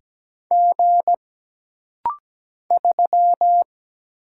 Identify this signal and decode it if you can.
Morse code, heard as a keyed tone, about 17 wpm, 700 Hz. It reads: G 3
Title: Single Letter-Number - Rapid-Fire 17wpm